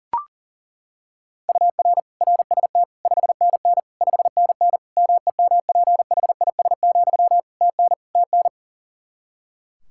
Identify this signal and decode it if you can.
Morse code, heard as a keyed tone, about 40 wpm, 700 Hz. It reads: UR RST 5NN 5NN MEMPHIS, TN TN
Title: Practice QSO - Exchange 1 - Beginning - 40wpm